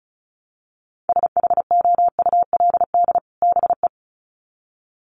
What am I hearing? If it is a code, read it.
Morse code, heard as a keyed tone, about 34 wpm, 700 Hz. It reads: SHOULD BE
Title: Sets of 2 Words 35wpm